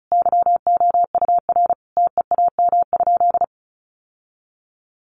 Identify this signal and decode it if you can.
Morse code, heard as a keyed tone, about 34 wpm, 700 Hz. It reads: YOUR TEAM?
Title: Sentences from Top 1500 Words 35wpm